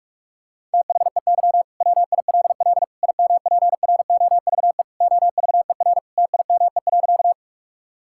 Morse code, heard as a keyed tone, about 46 wpm, 690 Hz. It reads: THEY WILL IMPROVE OVER TIME.
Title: Sentences from Top 1000 Words - Review Part 5 - 45wpm